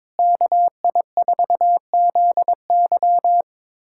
Morse code, heard as a keyed tone, about 22 wpm, 700 Hz. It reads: KI4ZY
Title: USA Callsign DE Callsign - 5 Characters - 22wpm